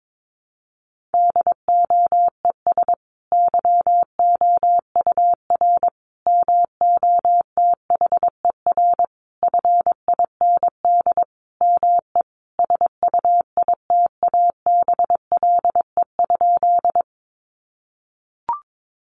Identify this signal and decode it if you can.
Morse code, heard as a keyed tone, about 22 wpm, 700 Hz. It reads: DOES YOUR MOTHER FIND ME SUITABLE?
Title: Sentences from Top 1400 Words 22wpm